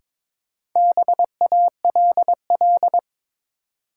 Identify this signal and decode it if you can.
Morse code, heard as a keyed tone, about 22 wpm, 700 Hz. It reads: BALL
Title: Sentences from Top 900 Words - New Word Drill - 22wpm